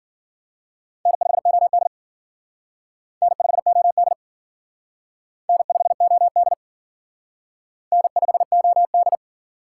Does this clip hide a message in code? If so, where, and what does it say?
Yes: Morse code from 1.1 to 9.2 s, heard as a keyed tone, about 51 wpm, 700 Hz. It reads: N5OD N5OD N5OD N5OD